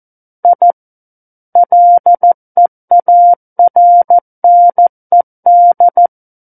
Morse code, heard as a keyed tone, about 14 wpm, 700 Hz. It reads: I LEARNED
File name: Book One / 05-Chapter 5 - 14 WPM